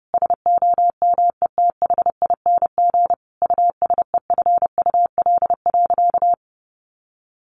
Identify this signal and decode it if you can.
Morse code, heard as a keyed tone, about 30 wpm, 700 Hz. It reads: SOMETHING USEFUL.
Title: Sentences from Top 1400 Words 30wpm